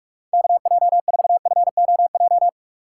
Morse code, heard as a keyed tone, about 49 wpm, 700 Hz. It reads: KJ4FOJ